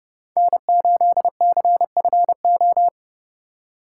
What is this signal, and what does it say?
Morse code, heard as a keyed tone, about 31 wpm, 700 Hz. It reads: N8CFO
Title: USA Call Signs - 5 Characters - 3x Character Spacing - 30wpm